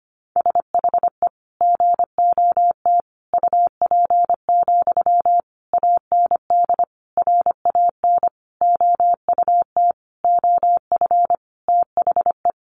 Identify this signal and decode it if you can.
Morse code, heard as a keyed tone, about 25 wpm, 700 Hz. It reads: SHE GOT UP, AND RAN OUT OF THE
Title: Sentences from Top 500 Words 25wpm